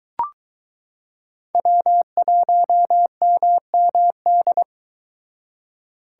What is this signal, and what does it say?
Morse code, heard as a keyed tone, about 22 wpm, 700 Hz. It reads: W1MMD